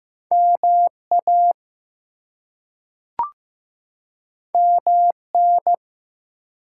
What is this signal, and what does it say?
Morse code, heard as a keyed tone, about 15 wpm, 700 Hz. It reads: MA MN
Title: US States and Canadian Provinces 15wpm